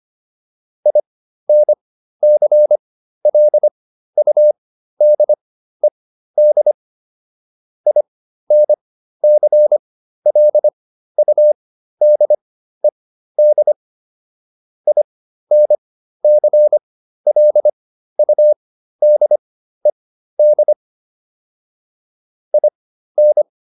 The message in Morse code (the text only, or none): INCLUDED INCLUDED INCLUDED IN